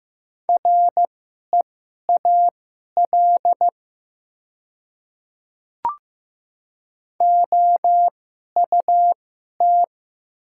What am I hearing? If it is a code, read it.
Morse code, heard as a keyed tone, about 15 wpm, spaced as for 11 wpm, 700 Hz. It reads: REAL OUT
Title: 2x Character Spacing - Top 300 Words - 15wpm